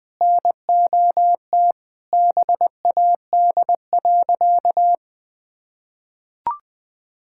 Morse code, heard as a keyed tone, about 21 wpm, 700 Hz. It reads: NOT BAD.
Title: Sentences from Top 1000 Words - Review Part 6 - 20wpm